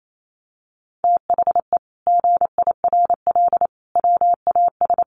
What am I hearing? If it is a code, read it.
Morse code, heard as a keyed tone, about 28 wpm, 700 Hz. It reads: THE GIRL WAS
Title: Sentences from Top 1000 Words 28wpm